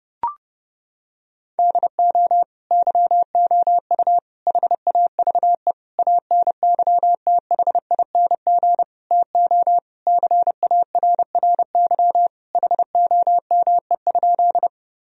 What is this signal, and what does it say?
Morse code, heard as a keyed tone, about 32 wpm, 700 Hz. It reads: DO YOU HAVE ANYTHING TO CARRY HOME?
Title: Sentences from Top 1000 Words - Review Part 1 - 30wpm